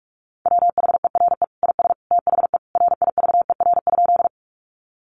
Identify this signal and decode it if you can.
Morse code, heard as a keyed tone, about 44 wpm, 700 Hz. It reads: WHERE IS THE RIVER?
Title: Sentences from Top 300 Words 45wpm